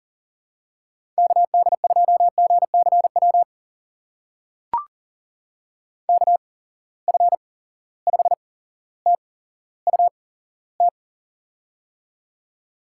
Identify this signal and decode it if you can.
Morse code, heard as a keyed tone, about 41 wpm, spaced as for 17 wpm, 700 Hz. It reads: KD2GCW KF5TUT
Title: USA Call Signs - 6 Characters - 8x Character Spacing - 40wpm